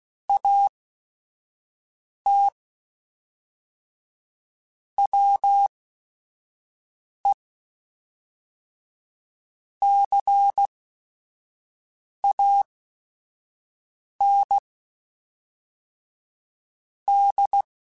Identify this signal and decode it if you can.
Morse code, heard as a keyed tone, about 16 wpm, spaced as for 5 wpm, 800 Hz. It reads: AT WE CAN D